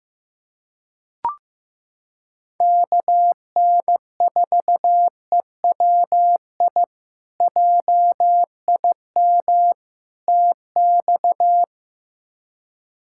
Morse code, heard as a keyed tone, about 15 wpm, 700 Hz. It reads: KN4EWI JIM TX
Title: K1USN SST QSO Practice - 15wpm